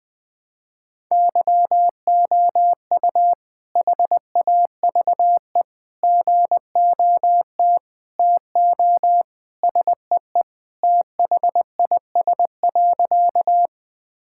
Morse code, heard as a keyed tone, about 20 wpm, 700 Hz. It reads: YOU HAVE GOT TO SEE THIS.
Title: Sentences with 2-Word Phrases - Spoken Beginning - 20wpm